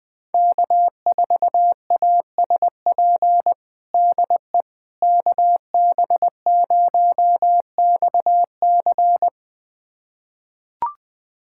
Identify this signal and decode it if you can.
Morse code, heard as a keyed tone, about 21 wpm, 700 Hz. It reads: K4ASP DE KB0XC
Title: USA Callsign DE Callsign - 5 Characters - 20wpm